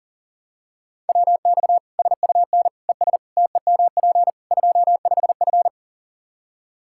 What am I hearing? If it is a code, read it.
Morse code, heard as a keyed tone, about 40 wpm, 700 Hz. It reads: WX SUN ES TEMP 25F
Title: Practice QSO - Exchange 2 - Info - 40wpm